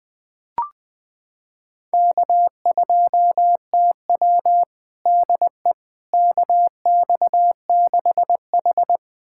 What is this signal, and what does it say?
Morse code, heard as a keyed tone, about 20 wpm, 700 Hz. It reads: K2TW DE KX6H
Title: USA Callsign DE Callsign - 4 Characters - 20wpm